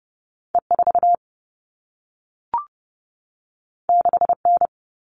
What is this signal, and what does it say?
Morse code, heard as a keyed tone, about 30 wpm, 700 Hz. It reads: E 4 6 N
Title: All Cut Numbers - 2-Digit 30wpm